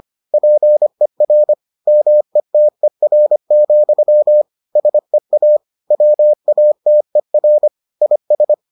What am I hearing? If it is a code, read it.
Morse code, heard as a keyed tone, about 25 wpm, 600 Hz. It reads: PER METER, SEA WATER IS